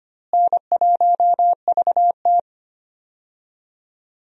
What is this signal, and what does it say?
Morse code, heard as a keyed tone, about 25 wpm, 700 Hz. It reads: N1VT
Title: USA Call Signs - 4 Characters - Repeated 2x - 25wpm